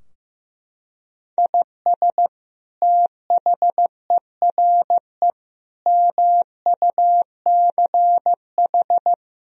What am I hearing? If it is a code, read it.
Morse code, heard as a keyed tone, about 15 wpm, 700 Hz. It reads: IS THERE MUCH
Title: Sentences from Top 1000 Words - Review Part 4 - 15wpm